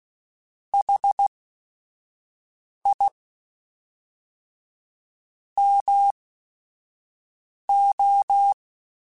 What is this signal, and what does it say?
Morse code, heard as a keyed tone, about 16 wpm, spaced as for 5 wpm, 800 Hz. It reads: HI MO
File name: QSOs / QSOs-mp3s / WA7QOX 1@5